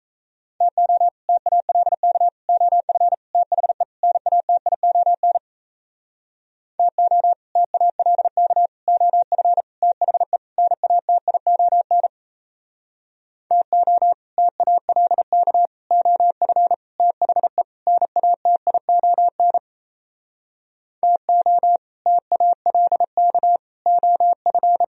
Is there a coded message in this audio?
Yes: Morse code from 0.6 to 24.8 s, heard as a keyed tone, about 38 wpm, 700 Hz. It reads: TO TALK OF THE NATION TO TALK OF THE NATION TO TALK OF THE NATION TO TALK OF